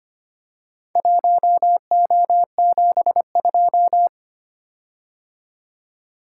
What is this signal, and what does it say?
Morse code, heard as a keyed tone, about 25 wpm, 700 Hz. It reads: 1O72
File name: practice / 25wpm / Four Letter-Number Combinations - 25wpm